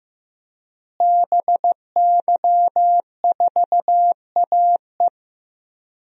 Morse code, heard as a keyed tone, about 15 wpm, 700 Hz. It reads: BY4AE
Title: DXCC-100 Call Signs - 5 Characters - Repeated 2x - 15wpm